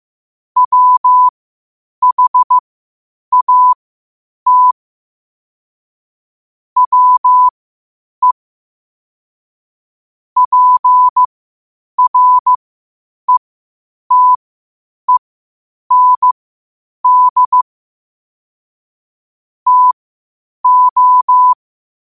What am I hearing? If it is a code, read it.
Morse code, heard as a keyed tone, about 15 wpm, spaced as for 8 wpm, 1000 Hz. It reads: WHAT WE PRETEND TO
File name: qotdfiles / 8wpm